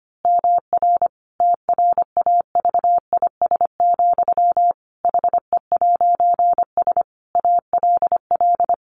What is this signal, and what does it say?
Morse code, heard as a keyed tone, about 25 wpm, 700 Hz. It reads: MR TRAVIS, HE'S ALL
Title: A Sound Of Thunder 25wpm